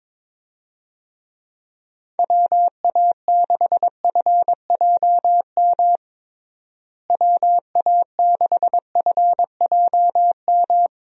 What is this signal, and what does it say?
Morse code, heard as a keyed tone, about 22 wpm, 700 Hz. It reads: WA6FJM WA6FJM
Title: USA Call Signs - 6 Characters - Repeated 2x - 22wpm